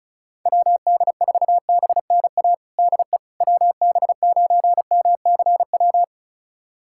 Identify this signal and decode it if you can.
Morse code, heard as a keyed tone, about 36 wpm, 700 Hz. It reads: WD4BNA DE WB9MCW
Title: USA Callsign DE Callsign - 6 Characters - 35wpm